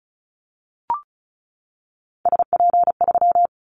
Morse code, heard as a keyed tone, about 34 wpm, 700 Hz. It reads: SP3
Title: Callsign Prefixes for Top 25 DXCC - 35wpm